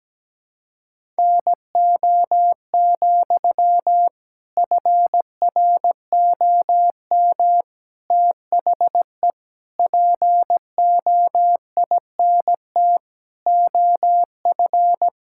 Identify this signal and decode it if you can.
Morse code, heard as a keyed tone, about 17 wpm, 700 Hz. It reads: NO, FROM THE POINT OF
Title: Sentences from Top 1000 Words - Review Part 6 - 17wpm